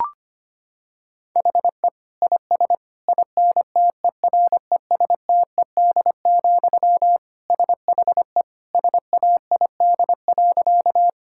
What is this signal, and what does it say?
Morse code, heard as a keyed tone, about 25 wpm, 700 Hz. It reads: HE IS INTERESTED, SHE SAID.
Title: Sentences from Top 1000 Words - Review Part 6 - 25wpm